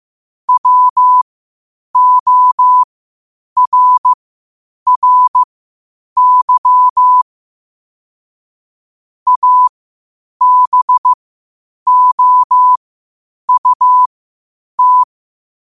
Morse code, heard as a keyed tone, about 15 wpm, spaced as for 8 wpm, 1000 Hz. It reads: WORRY ABOUT
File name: qotdfiles / 8wpm